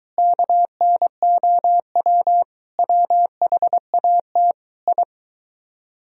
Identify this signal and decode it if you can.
Morse code, heard as a keyed tone, about 20 wpm, 700 Hz. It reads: KNOW WHAT I